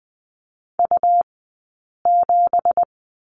Morse code, heard as a keyed tone, about 20 wpm, spaced as for 8 wpm, 700 Hz. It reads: U7
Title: 197 Lesson 32d - Letter k - Two Characters 20wpm